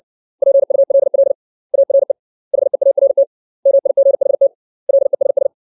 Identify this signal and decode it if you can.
Morse code, heard as a keyed tone, about 60 wpm, 550 Hz. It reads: PULL ARE HEART MIGHT LESS
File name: 632 words / 632 Words 60WPM